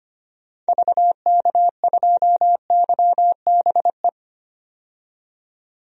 Morse code, heard as a keyed tone, about 25 wpm, 700 Hz. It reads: VK2YBE
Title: DXCC-100 Call Signs - 6 Characters - 25wpm